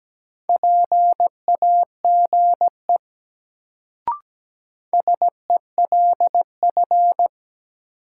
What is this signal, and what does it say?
Morse code, heard as a keyed tone, about 17 wpm, 700 Hz. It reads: PAGE SELF
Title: Sentences from Top 1000 Words - Review Part 1 - New Word Drill - 17wpm